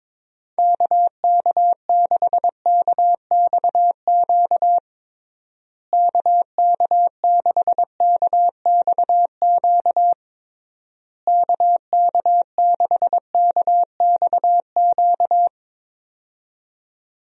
Morse code, heard as a keyed tone, about 22 wpm, 700 Hz. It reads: KK6KXQ KK6KXQ KK6KXQ
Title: USA Call Signs - 6 Characters - Repeated 3x - 22wpm